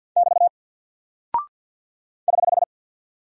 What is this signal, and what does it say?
Morse code, heard as a keyed tone, about 50 wpm, 700 Hz. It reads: <BT> <HH>
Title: Single Character (Letter:Number:Punctuation:Prosign) 50wpm